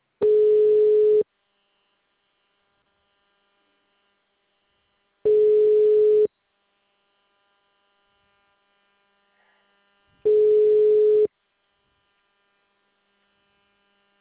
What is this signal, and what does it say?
Ringback tone (Europe, 425 Hz).